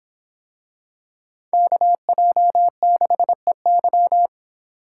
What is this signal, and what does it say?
Morse code, heard as a keyed tone, about 24 wpm, 700 Hz. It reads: KJ6EY